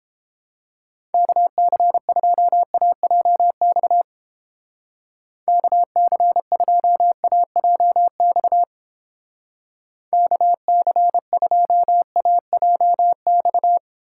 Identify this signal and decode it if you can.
Morse code, heard as a keyed tone, about 30 wpm, 700 Hz. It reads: KC2AJX KC2AJX KC2AJX